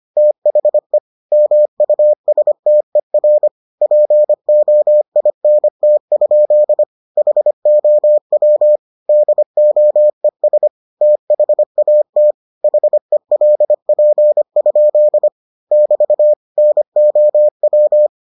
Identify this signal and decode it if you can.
Morse code, heard as a keyed tone, about 25 wpm, 600 Hz. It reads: THE MUSTER POINT? HOW DOES THAT HELP? <BT> NOW